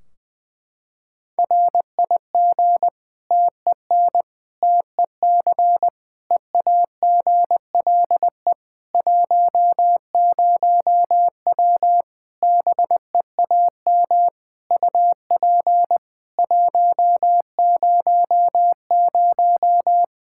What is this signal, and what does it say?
Morse code, heard as a keyed tone, about 20 wpm, 700 Hz. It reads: RIG TEN TEC EAGLE 10W BEAM UP 100
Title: Practice QSO - Exchange 2 - Info - 20wpm